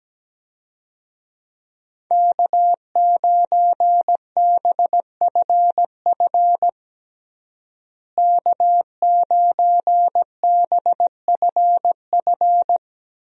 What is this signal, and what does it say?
Morse code, heard as a keyed tone, about 17 wpm, 700 Hz. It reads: K9BFF K9BFF
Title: USA Call Signs - 5 Characters - Repeated 2x - 17wpm